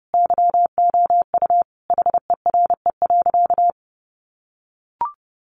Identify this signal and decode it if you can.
Morse code, heard as a keyed tone, about 30 wpm, 700 Hz. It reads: YOU HERE.
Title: A Sound Of Thunder 30wpm